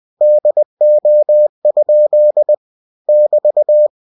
Morse code, heard as a keyed tone, about 20 wpm, 600 Hz. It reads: DO? <BT>
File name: cw-books / winnie-the-pooh / winnie-20wpm-page-0092